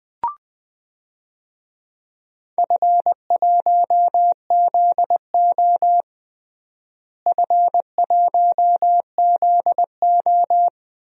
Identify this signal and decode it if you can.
Morse code, heard as a keyed tone, about 21 wpm, 700 Hz. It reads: F1ZO F1ZO
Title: DXCC-100 Call Signs - 4 Characters - Repeated 2x - 20wpm